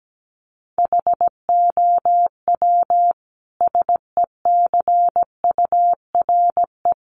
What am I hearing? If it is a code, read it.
Morse code, heard as a keyed tone, about 17 wpm, 700 Hz. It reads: HOW SECURE
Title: Sentences from Top 1200 Words 17wpm